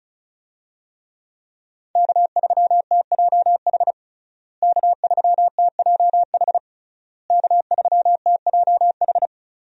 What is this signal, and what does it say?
Morse code, heard as a keyed tone, about 36 wpm, 700 Hz. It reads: K3TJH K3TJH K3TJH
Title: USA Call Signs - 5 Characters - Repeated 3x - 35wpm